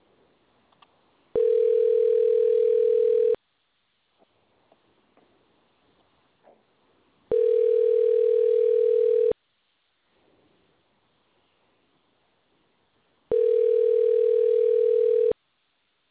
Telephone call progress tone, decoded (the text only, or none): ringback tone